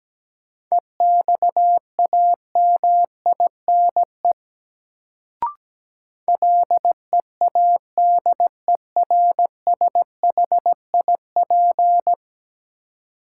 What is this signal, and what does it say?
Morse code, heard as a keyed tone, about 17 wpm, 700 Hz. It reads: EXAMINE LEADERSHIP
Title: Sentences from Top 1600 Words - New Word Drill - 17wpm